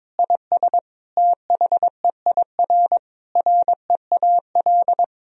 Morse code, heard as a keyed tone, about 22 wpm, 700 Hz. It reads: IS THEIR REAL
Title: Sentences from Top 1000 Words - Review Part 1 - 22wpm